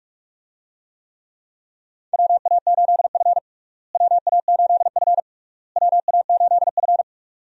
Morse code, heard as a keyed tone, about 46 wpm, 700 Hz. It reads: WA8F WA8F WA8F
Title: USA Call Signs - 4 Characters - Repeated 3x - 45wpm